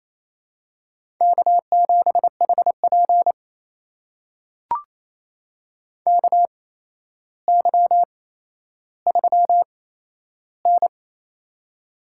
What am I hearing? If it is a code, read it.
Morse code, heard as a keyed tone, about 27 wpm, spaced as for 12 wpm, 700 Hz. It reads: K7HP KY3N
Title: USA Call Signs - 4 Characters - 8x Character Spacing - 28wpm